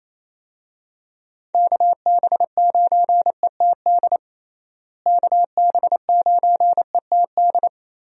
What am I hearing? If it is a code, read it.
Morse code, heard as a keyed tone, about 28 wpm, 700 Hz. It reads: KB9ETD KB9ETD
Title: USA Call Signs - 6 Characters - Repeated 3x - 28wpm